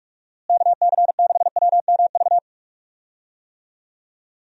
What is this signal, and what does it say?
Morse code, heard as a keyed tone, about 46 wpm, 700 Hz. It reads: KC6WMV
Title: USA Call Signs - 6 Characters - Repeated 3x - 45wpm